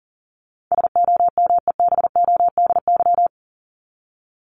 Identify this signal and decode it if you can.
Morse code, heard as a keyed tone, about 40 wpm, 700 Hz. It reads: SOMEBODY
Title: Top 1751-2000 Words 40wpm